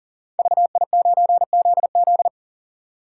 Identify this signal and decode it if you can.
Morse code, heard as a keyed tone, about 42 wpm, 700 Hz. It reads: VI9ZZ
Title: Code Groups - 5 Letter:Numbers - 40wpm